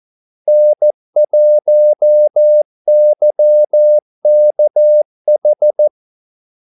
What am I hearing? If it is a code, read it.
Morse code, heard as a keyed tone, about 14 wpm, 600 Hz. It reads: N1YKH
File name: Callsigns-14wpm